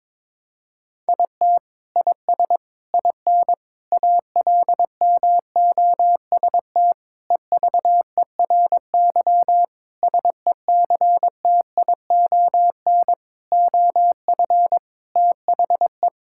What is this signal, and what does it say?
Morse code, heard as a keyed tone, about 22 wpm, 700 Hz. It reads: IT IS IN ALMOST EVERY SECTION OF THE
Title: Sentences from Top 1000 Words - Review Part 5 - 22wpm